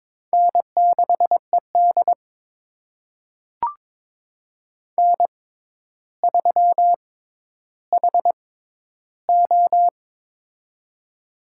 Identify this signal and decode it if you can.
Morse code, heard as a keyed tone, about 22 wpm, spaced as for 11 wpm, 700 Hz. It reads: N6ED N3HO
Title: USA Call Signs - 4 Characters - 6x Character Spacing - 22wpm